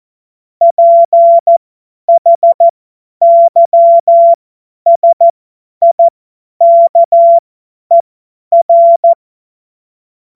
Morse code, heard as a keyed tone, about 14 wpm, spaced as for 10 wpm, 680 Hz. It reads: PHYSIKER
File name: Morsekurs / Witze / Witz 21. Juni 2022 50 F70